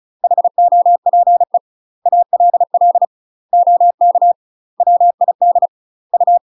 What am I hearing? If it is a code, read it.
Morse code, heard as a keyed tone, about 36 wpm, 700 Hz. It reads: HOPE ALL OK WID U
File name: UK MASTER QSOs TRAINING / 35WPM / UK MASTER QSOs 0023